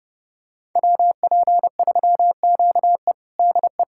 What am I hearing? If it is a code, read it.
Morse code, heard as a keyed tone, about 30 wpm, 700 Hz. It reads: WP3QE DE
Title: USA Callsign DE Callsign - 5 Characters - 30wpm